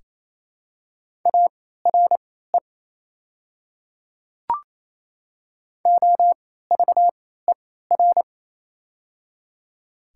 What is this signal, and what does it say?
Morse code, heard as a keyed tone, about 28 wpm, spaced as for 16 wpm, 700 Hz. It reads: ARE OVER
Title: 3x Character Spacing - Top 200 Words - 28wpm